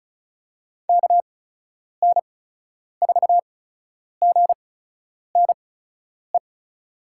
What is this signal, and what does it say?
Morse code, heard as a keyed tone, about 36 wpm, spaced as for 10 wpm, 700 Hz. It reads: KN4GNE
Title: USA Call Signs - 8x Character Spacing - 35wpm